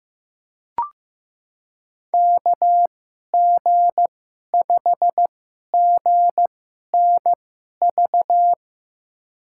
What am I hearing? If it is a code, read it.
Morse code, heard as a keyed tone, about 15 wpm, spaced as for 11 wpm, 700 Hz. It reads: KG5GNV
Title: USA Call Signs - 2x Character Spacing - 15wpm